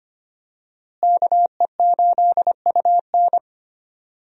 Morse code, heard as a keyed tone, about 25 wpm, 700 Hz. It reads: KE8UN
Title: USA Call Signs - 5 Characters - Repeated 2x - 25wpm